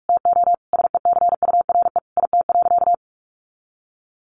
Morse code, heard as a keyed tone, about 44 wpm, 700 Hz. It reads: TO SECURE IT.
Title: Sentences from Top 1200 Words 45wpm